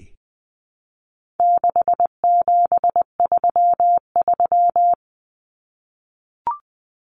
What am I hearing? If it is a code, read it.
Morse code, heard as a keyed tone, about 20 wpm, 700 Hz. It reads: 6733
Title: Numbers - 4-Digit 20wpm